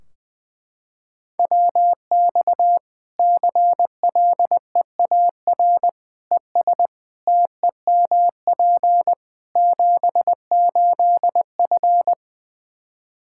Morse code, heard as a keyed tone, about 20 wpm, 700 Hz. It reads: WX CLEAR ES TEMP 78F
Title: Practice QSO - Exchange 2 - Info - 20wpm